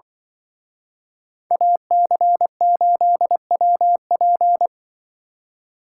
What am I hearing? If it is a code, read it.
Morse code, heard as a keyed tone, about 24 wpm, 700 Hz. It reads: AC8WP
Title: Speed Racing - USA 2x2 Call Signs 20wpm